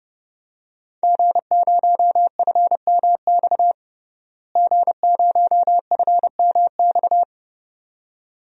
Morse code, heard as a keyed tone, about 30 wpm, 700 Hz. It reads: G0FMX G0FMX
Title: DXCC-100 Call Signs - 5 Characters - Repeated 2x - 30wpm